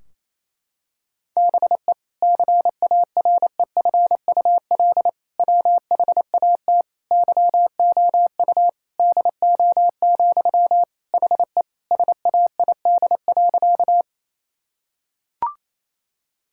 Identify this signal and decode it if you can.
Morse code, heard as a keyed tone, about 28 wpm, 700 Hz. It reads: BE CAREFUL WHAT YOU DO, HE SAID.
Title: Sentences from Top 1600 Words 28wpm